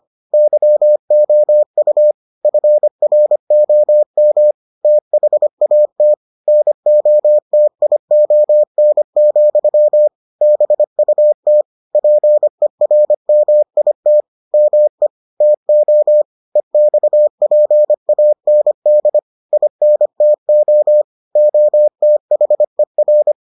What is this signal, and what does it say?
Morse code, heard as a keyed tone, about 25 wpm, 600 Hz. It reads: YOU FROM THAT NOTION, BUT PERMIT ME TO EXPAND INTO OTHER